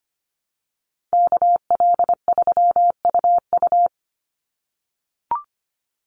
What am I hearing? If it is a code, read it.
Morse code, heard as a keyed tone, about 25 wpm, 700 Hz. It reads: KL3UU
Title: USA Call Signs 25wpm